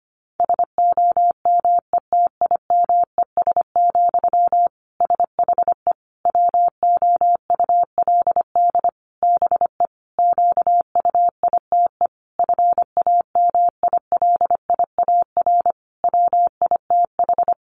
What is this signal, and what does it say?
Morse code, heard as a keyed tone, about 25 wpm, 700 Hz. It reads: SOMETIMES, SHE WOULD BE QUITE FAMILIAR WITH
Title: Sentences from Top 1000 Words 25wpm